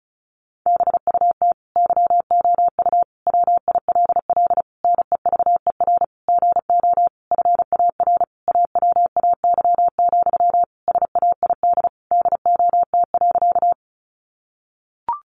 Morse code, heard as a keyed tone, about 34 wpm, 700 Hz. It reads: BUT YOU WILL NEVER GO FAR AWAY, SAID DOT.
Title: Sentences from Top 1500 Words 35wpm